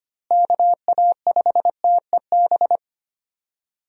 Morse code, heard as a keyed tone, about 25 wpm, 700 Hz. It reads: KA5TEB
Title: USA Call Signs - 6 Characters - Repeated 2x - 25wpm